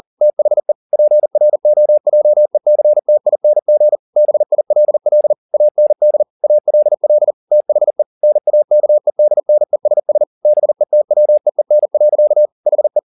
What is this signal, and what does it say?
Morse code, heard as a keyed tone, about 40 wpm, 600 Hz. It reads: THE PROJECTING BILL AND ALL THE NAKEDNESS BETWEEN. HE
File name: the old man and the sea 2 40wpm